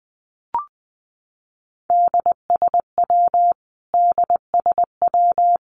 Morse code, heard as a keyed tone, about 20 wpm, 700 Hz. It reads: DSW DSW
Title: Abbreviations Repeated 3x - 20wpm